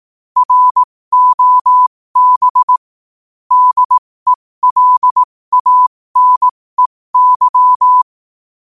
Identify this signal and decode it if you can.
Morse code, heard as a keyed tone, about 18 wpm, spaced as for 15 wpm, 1000 Hz. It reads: ROB DELANEY
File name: qotdfiles / 15wpm